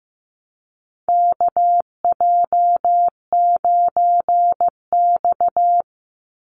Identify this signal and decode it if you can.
Morse code, heard as a keyed tone, about 15 wpm, 700 Hz. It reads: KJ9X
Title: NA SOTA Chaser Callsigns 15wpm